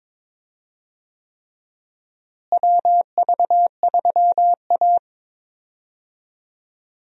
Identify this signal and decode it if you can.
Morse code, heard as a keyed tone, about 22 wpm, 700 Hz. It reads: WV3A
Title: USA Call Signs - 4 Characters - Repeated 3x - 22wpm